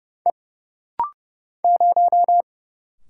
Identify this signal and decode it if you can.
Morse code, heard as a keyed tone, about 30 wpm, 700 Hz. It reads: E 0
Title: Single Letter-Number - Rapid-Fire 30wpm